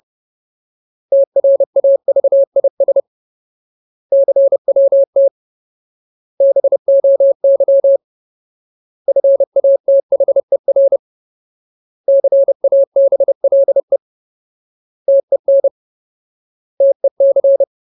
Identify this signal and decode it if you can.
Morse code, heard as a keyed tone, about 30 wpm, 550 Hz. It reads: TRAVIS CWT BOY FATHER CABLE TEN TEC
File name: Ultimate Word List / 30WPM Ultimate